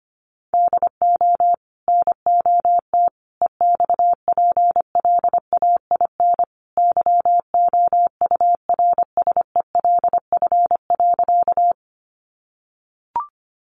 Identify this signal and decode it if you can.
Morse code, heard as a keyed tone, about 25 wpm, 700 Hz. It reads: DO NOT EXPLAIN YOURSELF.
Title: Sentences from Top 1100 Words 25wpm